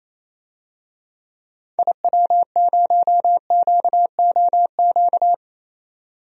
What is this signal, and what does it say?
Morse code, heard as a keyed tone, about 28 wpm, 700 Hz. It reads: IW0QOQ
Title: DXCC-100 Call Signs - 6 Characters - 28wpm